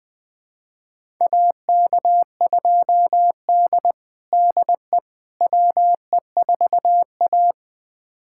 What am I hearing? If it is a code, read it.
Morse code, heard as a keyed tone, about 21 wpm, 700 Hz. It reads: AK2D DE WE4A
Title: USA Callsign DE Callsign - 4 Characters - 20wpm